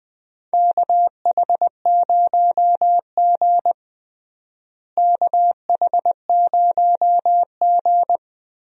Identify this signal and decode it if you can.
Morse code, heard as a keyed tone, about 20 wpm, 700 Hz. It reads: KH0G KH0G
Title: USA Call Signs - 4 Characters - Repeated 3x - 20wpm